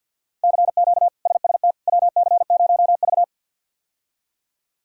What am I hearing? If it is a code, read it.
Morse code, heard as a keyed tone, about 52 wpm, 700 Hz. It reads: CQ SST WC0V
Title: K1USN SST QSO Practice - 50wpm